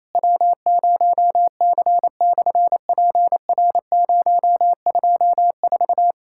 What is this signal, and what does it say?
Morse code, heard as a keyed tone, about 28 wpm, 700 Hz. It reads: W0C/PR024
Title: SOTA QSO Practice 28wpm